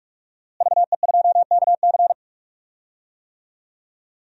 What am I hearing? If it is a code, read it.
Morse code, heard as a keyed tone, about 46 wpm, 700 Hz. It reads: VE2KC
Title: DXCC-100 Call Signs - 5 Characters - Repeated 3x - 45wpm